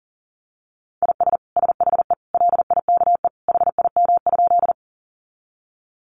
Morse code, heard as a keyed tone, about 40 wpm, 700 Hz. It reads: IS SHE LIKE HIM?
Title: Sentences from Top 100 Words 40wpm